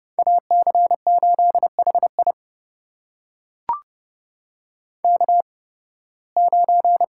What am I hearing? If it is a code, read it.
Morse code, heard as a keyed tone, about 32 wpm, spaced as for 20 wpm, 700 Hz. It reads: AC8HI K9
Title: USA Call Signs - 5 Characters - 8x Character Spacing - 30wpm